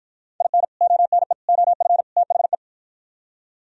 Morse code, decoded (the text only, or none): IN ONE OF THE